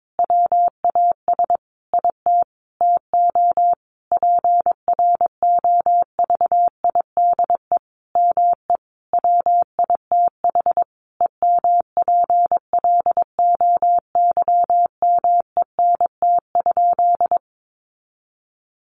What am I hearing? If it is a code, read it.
Morse code, heard as a keyed tone, about 22 wpm, 700 Hz. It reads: WAS IT TO PROVIDE ME WITH EMPLOYMENT?
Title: Sentences from Top 1500 Words 22wpm